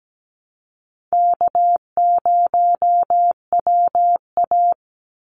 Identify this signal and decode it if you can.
Morse code, heard as a keyed tone, about 17 wpm, 700 Hz. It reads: K0WA
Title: Call Signs - 17wpm